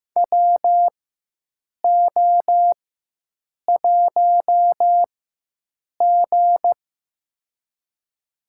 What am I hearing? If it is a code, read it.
Morse code, heard as a keyed tone, about 15 wpm, spaced as for 7 wpm, 700 Hz. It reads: WO1G
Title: USA Call Signs - 4 Characters - 4x Character Spacing - 15wpm